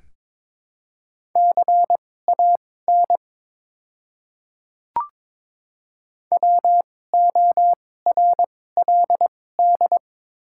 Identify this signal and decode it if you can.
Morse code, heard as a keyed tone, about 22 wpm, spaced as for 16 wpm, 700 Hz. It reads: CAN WORLD
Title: 2x Character Spacing - Top 300 Words - 22wpm